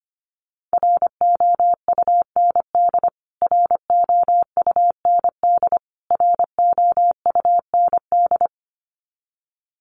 Morse code, heard as a keyed tone, about 25 wpm, 700 Hz. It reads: ROUND ROUND ROUND
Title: Top 200 Words Repeated 25wpm